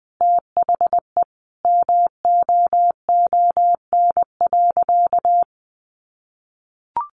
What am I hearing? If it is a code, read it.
Morse code, heard as a keyed tone, about 20 wpm, 700 Hz. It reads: THE MOON.
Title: A Sound Of Thunder 20wpm